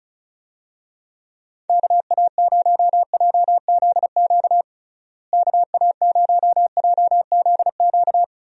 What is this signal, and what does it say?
Morse code, heard as a keyed tone, about 36 wpm, 700 Hz. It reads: KA0JZQ KA0JZQ
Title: USA Call Signs - 6 Characters - Repeated 3x - 35wpm